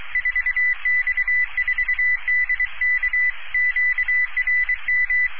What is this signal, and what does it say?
Morse code, heard as a keyed tone, about 23 wpm, 2000 Hz. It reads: 4X4DK QRM